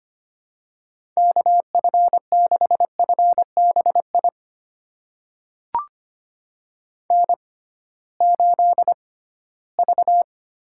Morse code, heard as a keyed tone, about 25 wpm, spaced as for 16 wpm, 700 Hz. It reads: KF6FBI N8V
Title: USA Call Signs - 6x Character Spacing - 25wpm